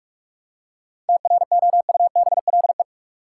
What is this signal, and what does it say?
Morse code, heard as a keyed tone, about 43 wpm, 700 Hz. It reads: TROUBLE
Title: Seven Letter Words - 45wpm